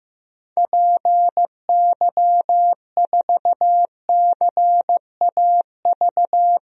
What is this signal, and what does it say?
Morse code, heard as a keyed tone, about 15 wpm, 700 Hz. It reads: PY4CAV
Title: DXCC-100 Call Signs - 6 Characters - 15wpm